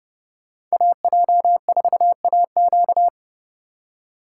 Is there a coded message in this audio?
Yes: Morse code from 0.7 to 3.1 s, heard as a keyed tone, about 38 wpm, 700 Hz. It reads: AJ4AQ